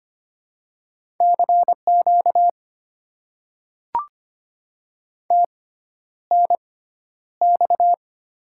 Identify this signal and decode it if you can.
Morse code, heard as a keyed tone, about 25 wpm, spaced as for 11 wpm, 700 Hz. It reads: CQ TNX
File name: Top 90 QSO Elements - 6x Character Spacing - 25wpm